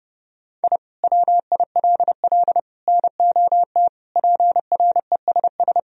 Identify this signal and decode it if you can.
Morse code, heard as a keyed tone, about 30 wpm, 700 Hz. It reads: I WILL NOT PRESS
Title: Sentences from Top 1000 Words - Review Part 1 - 30wpm